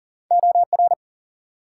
Morse code, heard as a keyed tone, about 42 wpm, 700 Hz. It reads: OR
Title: Two Letter Words - 40wpm